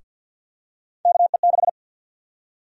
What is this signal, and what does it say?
Morse code, heard as a keyed tone, about 52 wpm, 700 Hz. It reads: KE6
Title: USA Call Sign Prefixes - 50wpm